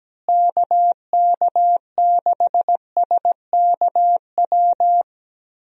Morse code, heard as a keyed tone, about 17 wpm, 700 Hz. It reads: KK6SKW
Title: USA Call Signs - 6 Characters - Repeated 3x - 17wpm